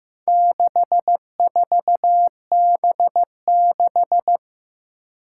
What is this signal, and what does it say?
Morse code, heard as a keyed tone, about 15 wpm, 700 Hz. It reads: 64B6
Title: Four Challenging Letter-Number Combinations 15wpm